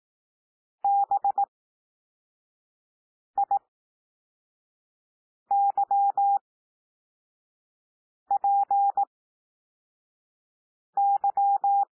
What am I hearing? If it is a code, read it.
Morse code, heard as a keyed tone, about 18 wpm, spaced as for 4 wpm, 800 Hz. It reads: BIYPY
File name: cwmp3 / L2 SOUND LETTERS 18 wpm